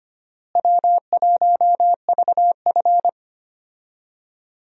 Morse code, heard as a keyed tone, about 25 wpm, 700 Hz. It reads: W1VF
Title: USA Call Signs - 4 Characters - Repeated 3x - 25wpm